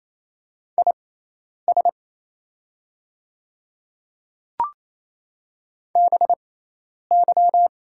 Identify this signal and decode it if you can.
Morse code, heard as a keyed tone, about 28 wpm, spaced as for 10 wpm, 700 Hz. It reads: IS BY
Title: 6x Character Spacing - Top 100 Words - 28wpm